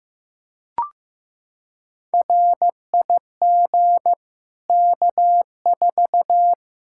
Morse code, heard as a keyed tone, about 16 wpm, 700 Hz. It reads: RIG K4
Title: Practice QSO - Exchange 2 - Info - 15wpm